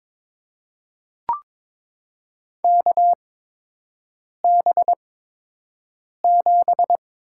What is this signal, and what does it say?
Morse code, heard as a keyed tone, about 22 wpm, spaced as for 6 wpm, 700 Hz. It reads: KB7
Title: USA Call Signs - 6 Characters - 8x Character Spacing - 22wpm